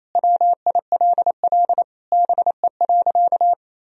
Morse code, heard as a keyed tone, about 28 wpm, 700 Hz. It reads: WILL BE.
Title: Sentences from Top 1000 Words - Review Part 4 - 28wpm